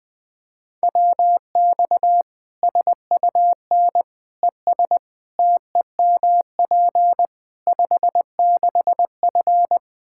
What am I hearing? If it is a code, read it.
Morse code, heard as a keyed tone, about 20 wpm, 700 Hz. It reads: WX SUN ES TEMP 56F
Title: Practice QSO - Exchange 2 - Info - 20wpm